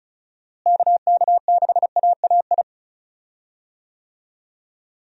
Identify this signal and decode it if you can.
Morse code, heard as a keyed tone, about 36 wpm, 700 Hz. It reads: KK6AAI
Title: USA Call Signs - 6 Characters - Repeated 3x - 35wpm